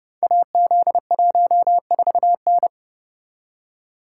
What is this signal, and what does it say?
Morse code, heard as a keyed tone, about 30 wpm, 700 Hz. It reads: AZ14N
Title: Code Groups - 5 Letter:Numbers - 30wpm